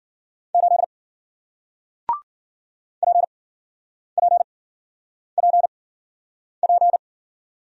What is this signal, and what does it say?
Morse code, heard as a keyed tone, about 54 wpm, spaced as for 9 wpm, 700 Hz. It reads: 8 PPPP